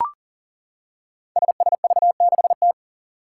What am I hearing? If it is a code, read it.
Morse code, heard as a keyed tone, about 37 wpm, 700 Hz. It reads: SSV6T
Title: Five Challenging Letter-Number Combinations 40wpm